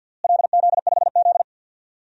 Morse code, heard as a keyed tone, about 53 wpm, 700 Hz. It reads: LZ5Z
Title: DXCC-100 Call Signs - 4 Characters - Repeated 3x - 50wpm